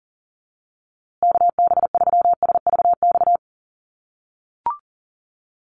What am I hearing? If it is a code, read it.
Morse code, heard as a keyed tone, about 40 wpm, 700 Hz. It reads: KB3SVX
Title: USA Call Signs Repeated 2x 40wpm